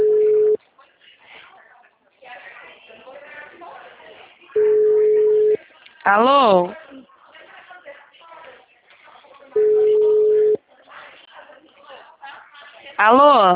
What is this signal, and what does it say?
Ringback tone (Europe, 425 Hz), with speech.
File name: out- -2010- - - .23257